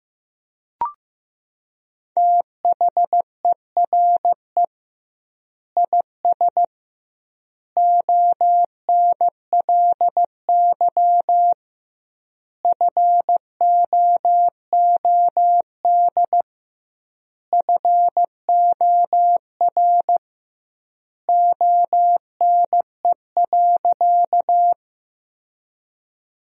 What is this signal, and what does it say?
Morse code, heard as a keyed tone, about 15 wpm, 700 Hz. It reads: THERE IS ONLY FOOD FOR ONE.
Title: 2x Word-Spacing - Sentences from Top 300 Words - 15wpm